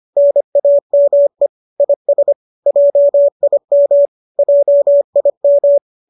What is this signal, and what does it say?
Morse code, heard as a keyed tone, about 25 wpm, 570 Hz. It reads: NAME IS JIM JIM
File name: qso203 25